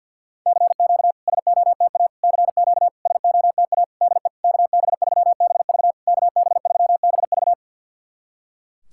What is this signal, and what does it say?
Morse code, heard as a keyed tone, about 51 wpm, 690 Hz. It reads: CQ SOTA CQ SOTA DE KB3BV KB3BV
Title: SOTA QSO Practice 50wpm